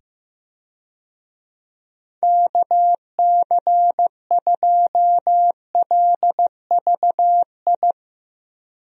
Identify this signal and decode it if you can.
Morse code, heard as a keyed tone, about 15 wpm, 700 Hz. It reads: KC2LVI
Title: USA Call Signs - 6 Characters - 15wpm